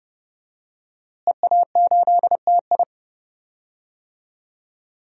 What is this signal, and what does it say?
Morse code, heard as a keyed tone, about 30 wpm, 700 Hz. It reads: EA8TI
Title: DXCC-100 Call Signs - 5 Characters - Repeated 2x - 30wpm